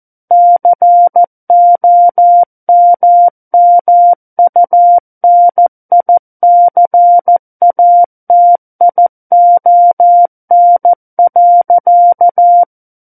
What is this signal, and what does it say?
Morse code, heard as a keyed tone, about 14 wpm, 700 Hz. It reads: COMMUNICATION.
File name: Book One / 05-Chapter 5 - 14 WPM